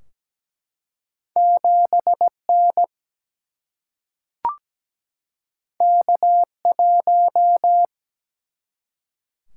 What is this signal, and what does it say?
Morse code, heard as a keyed tone, about 17 wpm, 700 Hz. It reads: 7 N K 1
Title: Single-Letter Single-Number 17wpm